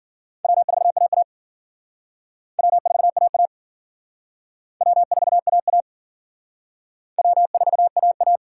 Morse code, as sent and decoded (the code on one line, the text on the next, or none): .-- ....- .- .- / .-- ....- .- .- / .-- ....- .- .- / .-- ....- .- .-
W4AA W4AA W4AA W4AA